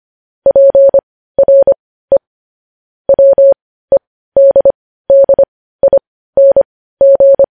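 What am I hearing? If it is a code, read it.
Morse code, heard as a keyed tone, about 25 wpm, spaced as for 15 wpm, 550 Hz. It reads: PRE WEDDING
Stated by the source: Morse code transcription of news headlines.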